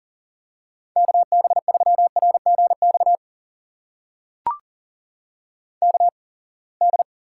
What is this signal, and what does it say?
Morse code, heard as a keyed tone, about 38 wpm, spaced as for 27 wpm, 700 Hz. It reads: KB3RGX KD
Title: USA Call Signs - 6 Characters - 8x Character Spacing - 40wpm